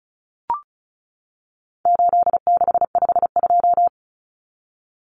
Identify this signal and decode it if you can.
Morse code, heard as a keyed tone, about 34 wpm, 700 Hz. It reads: 8652
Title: Common Cut Numbers - 4-Digit 35wpm